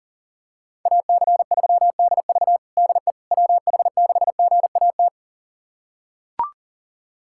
Morse code, heard as a keyed tone, about 40 wpm, 700 Hz. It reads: AC3DV DE WH6GAT
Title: USA Callsign DE Callsign 40wpm